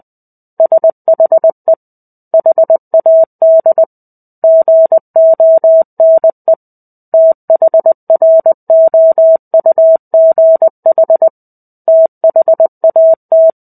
Morse code, heard as a keyed tone, about 20 wpm, 650 Hz. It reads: SHE HAD GONE THROUGH THAT
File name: 05-Chapter-05 20 WPM